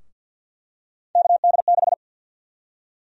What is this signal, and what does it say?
Morse code, heard as a keyed tone, about 46 wpm, 700 Hz. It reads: KD6
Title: USA Call Sign Prefixes - 50wpm